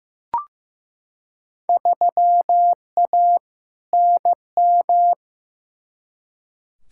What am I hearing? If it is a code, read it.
Morse code, heard as a keyed tone, about 15 wpm, 700 Hz. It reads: 3A NM
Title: Field Day Exchanges Encore - 15wpm